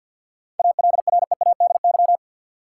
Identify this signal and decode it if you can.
Morse code, heard as a keyed tone, about 52 wpm, 700 Hz. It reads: ALREADY
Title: Sentences from Top 700 Words - New Word Drill - 50wpm